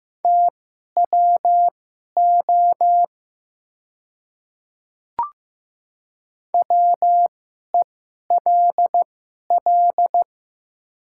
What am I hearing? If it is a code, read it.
Morse code, heard as a keyed tone, about 15 wpm, spaced as for 11 wpm, 700 Hz. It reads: TWO WELL
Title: 2x Character Spacing - Top 300 Words - 15wpm